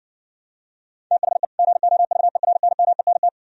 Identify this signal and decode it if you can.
Morse code, heard as a keyed tone, about 70 wpm, 700 Hz. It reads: THE GOVERNMENT